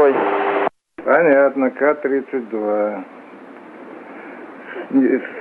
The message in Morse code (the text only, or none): none